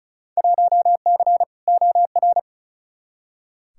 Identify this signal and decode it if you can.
Morse code, heard as a keyed tone, about 36 wpm, 700 Hz. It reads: 1C OR
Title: Field Day Exchanges - 35wpm